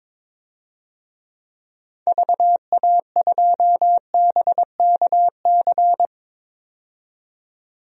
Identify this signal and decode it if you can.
Morse code, heard as a keyed tone, about 22 wpm, 700 Hz. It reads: VA2BKC
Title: DXCC-100 Call Signs - 6 Characters - 22wpm